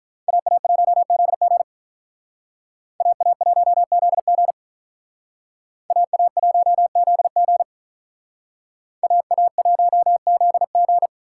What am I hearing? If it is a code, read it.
Morse code, heard as a keyed tone, about 42 wpm, 700 Hz. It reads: AA1ZG AA1ZG AA1ZG AA1ZG
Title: Speed Racing - USA 2x2 Call Signs 35wpm